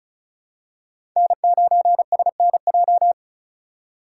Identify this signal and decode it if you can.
Morse code, heard as a keyed tone, about 34 wpm, 700 Hz. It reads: N9SNJ
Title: USA Call Signs - 4x Character Spacing - 35wpm